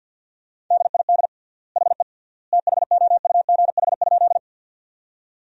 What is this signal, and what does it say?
Morse code, heard as a keyed tone, about 50 wpm, 700 Hz. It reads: DID HE THOUGH?
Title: 3x Word-Spacing - Sentences from Top 400 Words - 50wpm